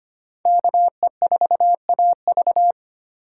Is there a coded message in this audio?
Yes: Morse code from 0.5 to 2.7 s, heard as a keyed tone, about 32 wpm, 700 Hz. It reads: KE4AV